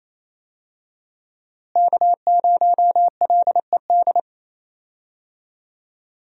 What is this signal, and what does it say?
Morse code, heard as a keyed tone, about 28 wpm, 700 Hz. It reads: K0LED
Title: USA Call Signs - 5 Characters - Repeated 2x - 28wpm